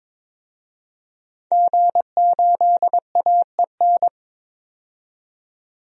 Morse code, heard as a keyed tone, about 21 wpm, 700 Hz. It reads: G8AEN
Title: DXCC-100 Call Signs - 5 Characters - 22wpm